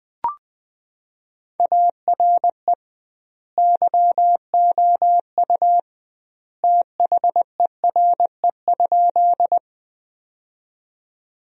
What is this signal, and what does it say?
Morse code, heard as a keyed tone, about 21 wpm, 700 Hz. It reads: ARE YOU THERE?
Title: 2x Word-Spacing - Sentences from Top 100 Words - 20wpm